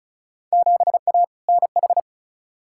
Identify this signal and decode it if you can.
Morse code, heard as a keyed tone, about 36 wpm, 700 Hz. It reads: 7A NH
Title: Field Day Exchanges - 35wpm